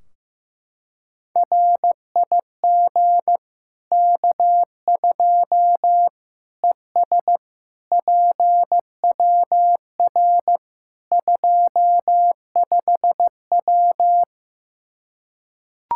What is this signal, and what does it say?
Morse code, heard as a keyed tone, about 15 wpm, 700 Hz. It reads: RIG K2 ES PWR 25W
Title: Practice QSO - Exchange 2 - Info - 15wpm